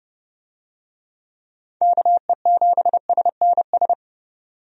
Morse code, heard as a keyed tone, about 30 wpm, 700 Hz. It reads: KE7SNS
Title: USA Call Signs - 6 Characters - 30wpm